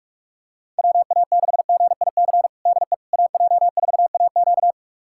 Morse code, heard as a keyed tone, about 46 wpm, 700 Hz. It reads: WA6GIC DE AJ4AQ
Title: USA Callsign DE Callsign 45wpm